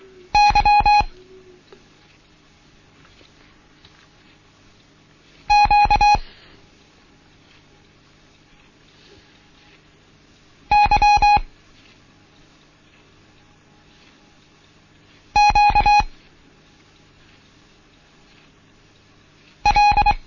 Morse code, heard as a keyed tone, about 24 wpm, 830 Hz. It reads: Y Q Y Q L
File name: Morsen / Kurs A 16